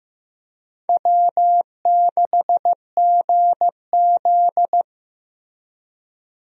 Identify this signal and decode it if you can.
Morse code, heard as a keyed tone, about 16 wpm, 700 Hz. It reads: W6GZ
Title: USA Call Signs - 4 Characters - Repeated 2x - 15wpm